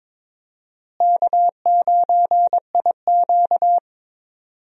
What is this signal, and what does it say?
Morse code, heard as a keyed tone, about 22 wpm, 700 Hz. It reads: K9IQ
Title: USA Call Signs - 4 Characters - Repeated 2x - 22wpm